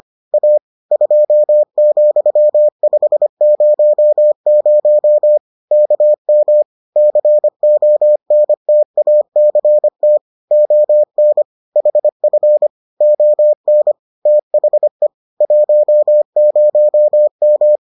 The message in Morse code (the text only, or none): A 2,500 KM CONTACT ON HF ON THE 10M